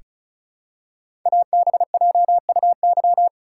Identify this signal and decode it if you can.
Morse code, heard as a keyed tone, about 36 wpm, 700 Hz. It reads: ABJUY
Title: Code Groups - 5 Letters - 35wpm